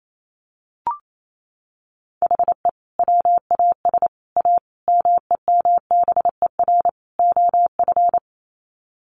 Morse code, heard as a keyed tone, about 28 wpm, 700 Hz. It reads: HE WAS A MEMBER OF
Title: Sets of 5 Words Encore2 28wpm